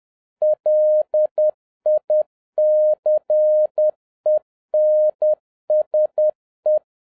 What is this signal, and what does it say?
Morse code, heard as a keyed tone, about 10 wpm, 610 Hz. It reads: LICENSE